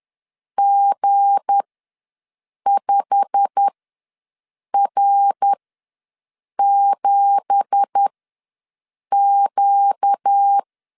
Morse code, heard as a keyed tone, about 10 wpm, spaced as for 6 wpm, 800 Hz. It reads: G5R7Q
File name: LEZIONI / 9 2867 ws 10wpm 4215